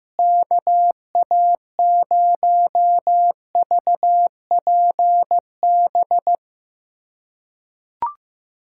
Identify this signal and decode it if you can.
Morse code, heard as a keyed tone, about 15 wpm, 700 Hz. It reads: KA0VPB
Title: USA Callsign DE Callsign - 6 Characters - 15wpm